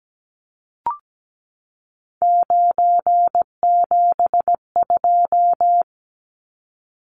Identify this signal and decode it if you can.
Morse code, heard as a keyed tone, about 17 wpm, 700 Hz. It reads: 972
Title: Numbers - 3-Digit 17wpm